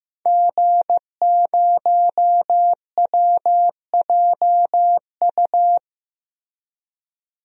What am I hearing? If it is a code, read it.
Morse code, heard as a keyed tone, about 16 wpm, 700 Hz. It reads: G0WJU
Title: DXCC-100 Call Signs - 5 Characters - Repeated 3x - 15wpm